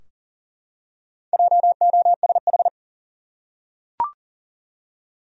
Morse code, heard as a keyed tone, about 40 wpm, 700 Hz. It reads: JOSH
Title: USA Common Names - Max 4 Letters - 40wpm